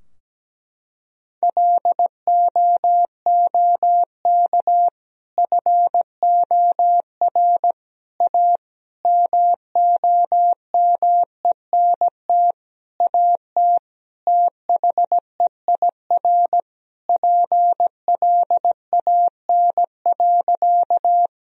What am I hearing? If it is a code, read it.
Morse code, heard as a keyed tone, about 17 wpm, 700 Hz. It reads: LOOK FOR A MOMENT AT THEIR PLAN.
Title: Sentences from Top 1000 Words - Review Part 2 - 17wpm